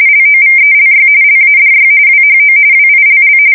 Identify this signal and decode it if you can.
RTTY, about 45.45 baud. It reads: LOVELY LADY WHO WILL
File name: fsk170 / battleshipnj